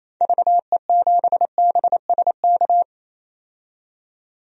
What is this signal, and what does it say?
Morse code, heard as a keyed tone, about 28 wpm, 700 Hz. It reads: VE7BSK
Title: DXCC-100 Call Signs - 6 Characters - 28wpm